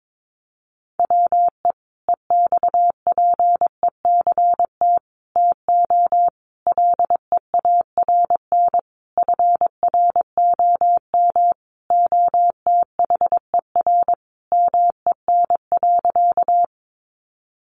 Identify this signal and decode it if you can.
Morse code, heard as a keyed tone, about 22 wpm, 700 Hz. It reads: WE EXPECT TO LEARN FROM OTHER MEN.
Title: Sentences from Top 1100 Words 22wpm